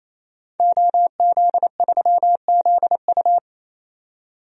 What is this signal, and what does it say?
Morse code, heard as a keyed tone, about 28 wpm, 700 Hz. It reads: OZ3ZU
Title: DXCC-100 Call Signs - 5 Characters - 28wpm